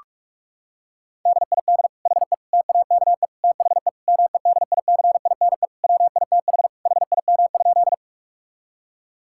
Morse code, heard as a keyed tone, about 41 wpm, 700 Hz. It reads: DID HE TAKE THE MEDICINE WITH HIM?
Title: Sentences from Top 1000 Words - Review Part 6 - 45wpm